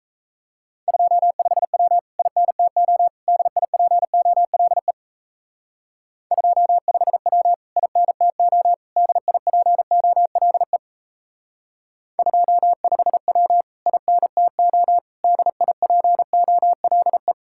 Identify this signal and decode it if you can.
Morse code, heard as a keyed tone, about 34 wpm, 700 Hz. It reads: 25W INTO DIPOLE 25W INTO DIPOLE 25W INTO DIPOLE
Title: Speed-Racing Common QSO Phrases Encore - 28wpm